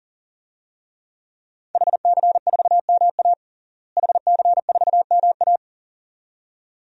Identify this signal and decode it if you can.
Morse code, heard as a keyed tone, about 40 wpm, 700 Hz. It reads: HC4MA HC4MA
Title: DXCC-100 Call Signs - 5 Characters - Repeated 2x - 40wpm